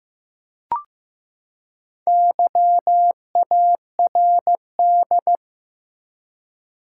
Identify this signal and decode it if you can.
Morse code, heard as a keyed tone, about 15 wpm, 700 Hz. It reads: YARD
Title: Four Letter Words - 15wpm